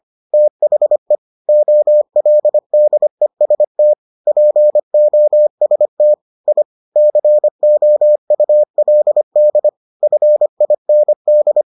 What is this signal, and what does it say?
Morse code, heard as a keyed tone, about 25 wpm, 600 Hz. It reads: THE OLDEST POST I COULD FIND